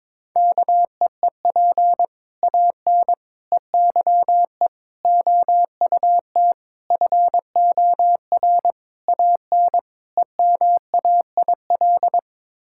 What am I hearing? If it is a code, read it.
Morse code, heard as a keyed tone, about 22 wpm, 700 Hz. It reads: KEEP AN EYE OUT FOR AN EMAIL
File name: practice / 22wpm / Sets of 7 Words - 22wpm